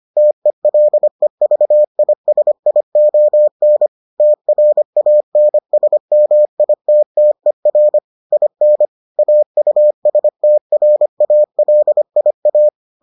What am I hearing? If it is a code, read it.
Morse code, heard as a keyed tone, about 25 wpm, 600 Hz. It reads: TELEVISION TRANSMITTER IN AUSTRALIA